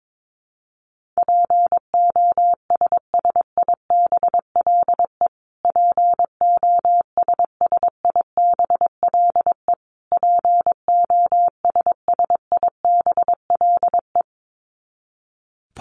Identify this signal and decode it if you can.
Morse code, heard as a keyed tone, about 22 wpm, 700 Hz. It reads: POSSIBLE POSSIBLE POSSIBLE
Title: Top 500 Words Repeated 22wpm